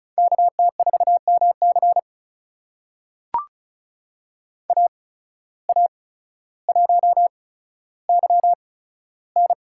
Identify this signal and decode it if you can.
Morse code, heard as a keyed tone, about 36 wpm, spaced as for 15 wpm, 700 Hz. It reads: KT4MC AA1YN
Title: USA Call Signs - 5 Characters - 8x Character Spacing - 35wpm